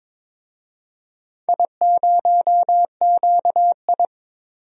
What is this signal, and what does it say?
Morse code, heard as a keyed tone, about 23 wpm, 700 Hz. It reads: I0QI
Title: DXCC-100 Call Signs - 4 Characters - 22wpm